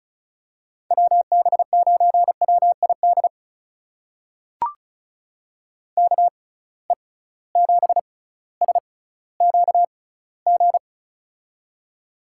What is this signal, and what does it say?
Morse code, heard as a keyed tone, about 36 wpm, spaced as for 18 wpm, 700 Hz. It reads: WB9WID KE7SQG
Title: USA Call Signs - 6 Characters - 6x Character Spacing - 35wpm